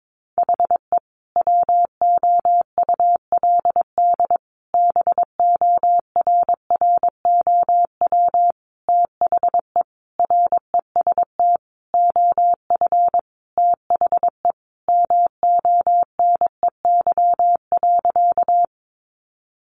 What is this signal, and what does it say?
Morse code, heard as a keyed tone, about 22 wpm, 700 Hz. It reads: HE WOULD BORROW THE REST OF THE MONEY.
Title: Sentences from Top 1500 Words 22wpm